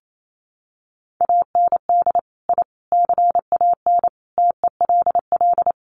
Morse code, heard as a keyed tone, about 28 wpm, 700 Hz. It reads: AND I CAN TELL
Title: Sets of 5 Words Encore2 28wpm